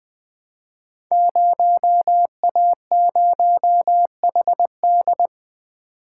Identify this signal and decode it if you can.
Morse code, heard as a keyed tone, about 20 wpm, 700 Hz. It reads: 0A0HD
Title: Code Groups - 5 Letter:Numbers - 20wpm